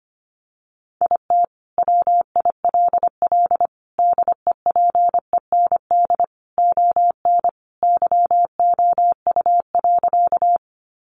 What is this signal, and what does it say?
Morse code, heard as a keyed tone, about 25 wpm, 700 Hz. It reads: IT WILL DEPEND ON YOU.
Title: Sentences from Top 1300 Words 25wpm